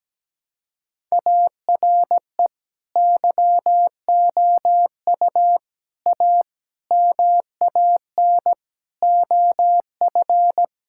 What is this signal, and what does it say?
Morse code, heard as a keyed tone, about 17 wpm, 700 Hz. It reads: ARE YOU A MAN OF
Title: Sentences from Top 1000 Words - Review Part 6 - 17wpm